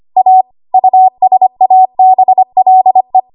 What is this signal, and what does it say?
Morse code, heard as a keyed tone, about 25 wpm, 750 Hz. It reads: A USABLE
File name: Archive / 25 WPM / 25WPM